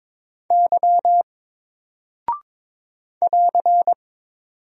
Morse code, heard as a keyed tone, about 22 wpm, 700 Hz. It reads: Y <AR>
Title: Single Character (Letter:Number:Punctuation:Prosign) Alt 22wpm